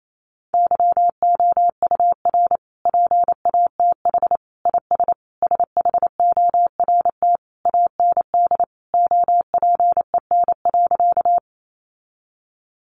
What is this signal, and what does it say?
Morse code, heard as a keyed tone, about 28 wpm, 700 Hz. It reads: YOUR PATH IS SHORT AND OPEN.
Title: Sentences from Top 1400 Words 28wpm